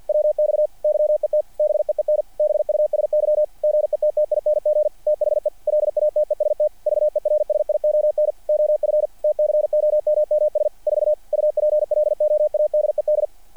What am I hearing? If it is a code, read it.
Morse code, heard as a keyed tone, about 48 wpm, 600 Hz. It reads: WX YET BEEN BUSY GETTING THE LATEST VERSION OF TCOMMS 4 UPLOADED
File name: archived / CRSnet